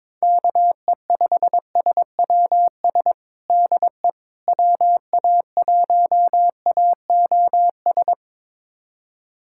Morse code, heard as a keyed tone, about 22 wpm, 700 Hz. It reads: KE5SWS DE WA1AOS
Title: USA Callsign DE Callsign - 6 Characters - 22wpm